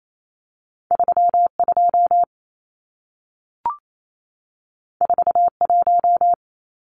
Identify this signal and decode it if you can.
Morse code, heard as a keyed tone, about 28 wpm, 700 Hz. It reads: 32 41
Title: Numbers - 2-Digit 28wpm